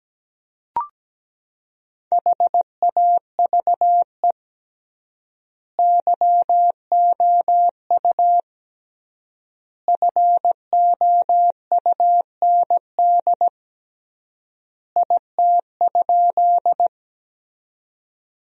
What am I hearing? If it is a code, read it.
Morse code, heard as a keyed tone, about 17 wpm, 700 Hz. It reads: HAVE YOU FOUND IT?
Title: 3x Word-Spacing - Sentences from Top 300 Words - 17wpm